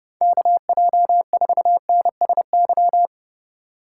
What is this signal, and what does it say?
Morse code, heard as a keyed tone, about 36 wpm, 700 Hz. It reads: KJ4NSY
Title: Speed Racing - USA Call Signs 30wpm